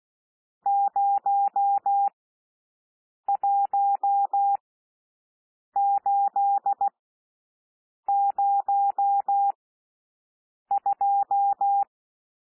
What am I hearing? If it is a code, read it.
Morse code, heard as a keyed tone, about 16 wpm, spaced as for 6 wpm, 800 Hz. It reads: 01802
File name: cwmp3 / N1 SOUND NUMBERS 16 wpm